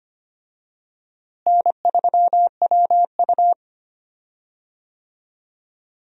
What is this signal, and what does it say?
Morse code, heard as a keyed tone, about 25 wpm, 700 Hz. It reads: N3WU
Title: USA Call Signs - 4 Characters - Repeated 2x - 25wpm